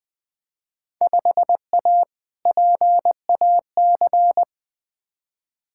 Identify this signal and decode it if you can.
Morse code, heard as a keyed tone, about 20 wpm, 700 Hz. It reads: 5A PAC
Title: Field Day Exchanges Encore - 20wpm